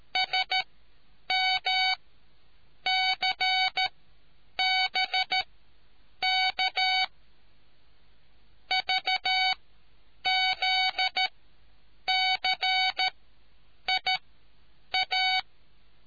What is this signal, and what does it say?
Morse code, heard as a keyed tone, about 13 wpm, spaced as for 8 wpm, 2250 Hz. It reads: SMCBK VZCIA